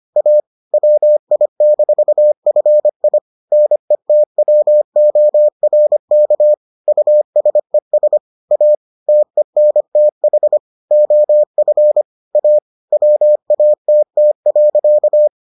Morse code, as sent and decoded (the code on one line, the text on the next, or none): .- / .-- .. -....- ..-. .. / -. - .-- --- .-. -.- / ..- ... ... / .- / - . -. - .... / --- ..-. / .- / .-- .- - - .-.-.-
A WI-FI NETWORK USES A TENTH OF A WATT.